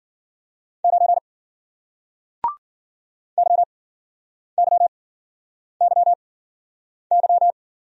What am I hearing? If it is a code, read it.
Morse code, heard as a keyed tone, about 50 wpm, spaced as for 9 wpm, 700 Hz. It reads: 9 YYYY